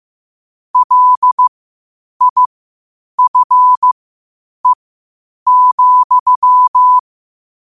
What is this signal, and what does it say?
Morse code, heard as a keyed tone, about 15 wpm, spaced as for 8 wpm, 1000 Hz. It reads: LIFE,
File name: qotdfiles / 8wpm